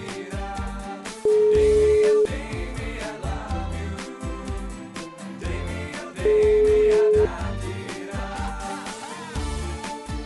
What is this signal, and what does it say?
Ringback tone (Europe, 425 Hz).